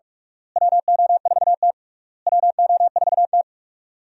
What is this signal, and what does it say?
Morse code, heard as a keyed tone, about 47 wpm, 700 Hz. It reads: WO4T WO4T
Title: USA Call Signs - 4 Characters - Repeated 3x - 45wpm